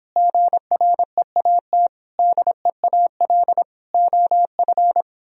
Morse code, heard as a keyed tone, about 26 wpm, 700 Hz. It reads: GREAT DEAL OF